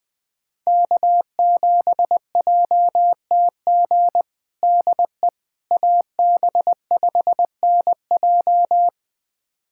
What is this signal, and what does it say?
Morse code, heard as a keyed tone, about 20 wpm, 700 Hz. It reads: K7JTG DE AB5NJ
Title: USA Callsign DE Callsign - 5 Characters - 20wpm